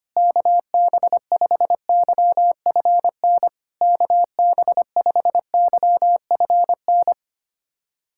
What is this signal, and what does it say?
Morse code, heard as a keyed tone, about 25 wpm, 700 Hz. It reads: KB5YFN KB5YFN
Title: SOTA QSO Practice 25wpm